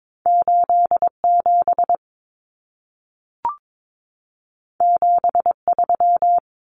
Morse code, heard as a keyed tone, about 22 wpm, 700 Hz. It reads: 87 73
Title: Common Cut Numbers - 2-Digit 22wpm